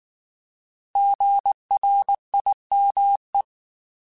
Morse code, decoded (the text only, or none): GRIME